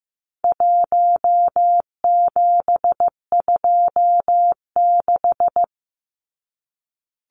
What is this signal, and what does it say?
Morse code, heard as a keyed tone, about 15 wpm, 700 Hz. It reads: 1726
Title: Numbers - 4-Digit 15wpm